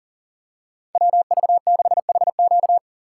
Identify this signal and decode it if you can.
Morse code, heard as a keyed tone, about 40 wpm, 700 Hz. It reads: WV6HQ
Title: USA Call Signs - 5 Characters - 2x Character Spacing - 40wpm